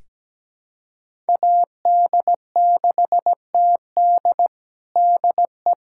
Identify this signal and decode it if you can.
Morse code, heard as a keyed tone, about 17 wpm, 700 Hz. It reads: AD6TD DE
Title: USA Callsign DE Callsign - 5 Characters - 17wpm